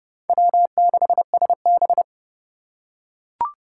Morse code, heard as a keyed tone, about 30 wpm, 700 Hz. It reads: W6SB
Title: USA Call Signs - 3x Character Spacing - 30wpm